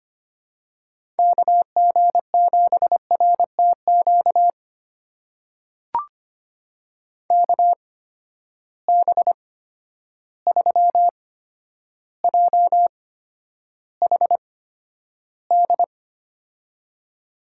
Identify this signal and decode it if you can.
Morse code, heard as a keyed tone, about 25 wpm, spaced as for 11 wpm, 700 Hz. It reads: KG7RTQ KB3JHD
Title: USA Call Signs - 6 Characters - 8x Character Spacing - 25wpm